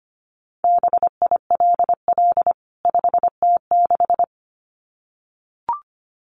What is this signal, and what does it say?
Morse code, heard as a keyed tone, about 25 wpm, 700 Hz. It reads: BILL 5T6
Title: CWT Exchange 25wpm